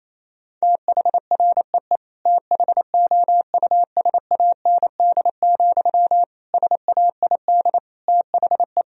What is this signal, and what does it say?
Morse code, heard as a keyed tone, about 28 wpm, 700 Hz. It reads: THREE THOUSAND, SAID THE
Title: Sentences from Top 1000 Words - Review Part 3 - 28wpm